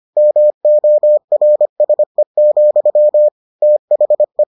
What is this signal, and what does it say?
Morse code, heard as a keyed tone, about 25 wpm, 600 Hz. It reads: MORSE, THE